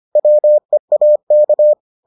Morse code, heard as a keyed tone, about 25 wpm, 600 Hz. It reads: WEAK